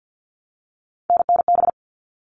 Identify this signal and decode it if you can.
Morse code, heard as a keyed tone, about 50 wpm, 700 Hz. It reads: NNB
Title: All Cut Numbers - 3-Digit 50wpm